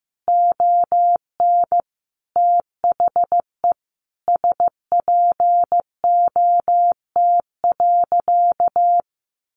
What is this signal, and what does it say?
Morse code, heard as a keyed tone, about 15 wpm, 700 Hz. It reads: ON THE SPOT.
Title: Sentences from Top 1200 Words 15wpm